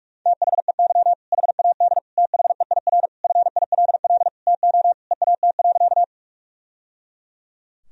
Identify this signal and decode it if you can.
Morse code, heard as a keyed tone, about 41 wpm, 700 Hz. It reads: THEY HAD THEIR FILL TO EAT.
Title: Sentences from Top 1000 Words - Review Part 3 - 45wpm